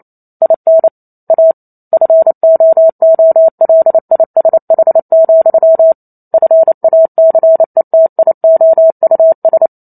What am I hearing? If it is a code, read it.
Morse code, heard as a keyed tone, about 28 wpm, 650 Hz. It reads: IN A FOOLISH, FACETIOUS